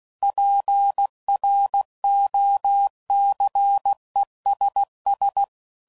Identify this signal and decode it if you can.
Morse code, heard as a keyed tone, about 16 wpm, 800 Hz. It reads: PROCESS
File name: CW Code Groups @15WPM / Technical Subjects / AND Gate